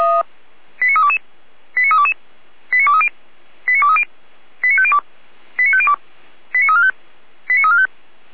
Five-tone selective call (ZVEI), heard as quick runs of five tones, tones of about 70 ms. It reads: 89322 89322 89230 89230 89502 89502 89345 89345